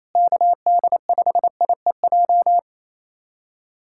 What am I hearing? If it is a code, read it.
Morse code, heard as a keyed tone, about 28 wpm, 700 Hz. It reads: KD5IEJ
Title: USA Callsign DE Callsign - 6 Characters - 28wpm